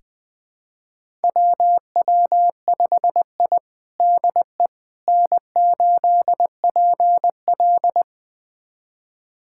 Morse code, heard as a keyed tone, about 20 wpm, 700 Hz. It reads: WW5I DE N8PL
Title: USA Callsign DE Callsign - 4 Characters - 20wpm